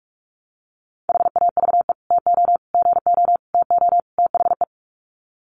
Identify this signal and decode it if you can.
Morse code, heard as a keyed tone, about 44 wpm, 700 Hz. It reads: HAVE TO GO TO THE
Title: Sets of 5 Words Encore 45wpm